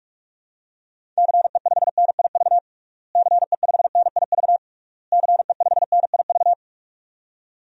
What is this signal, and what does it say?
Morse code, heard as a keyed tone, about 41 wpm, 700 Hz. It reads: CE5NIV CE5NIV CE5NIV
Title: DXCC-100 Call Signs - 6 Characters - Repeated 3x - 45wpm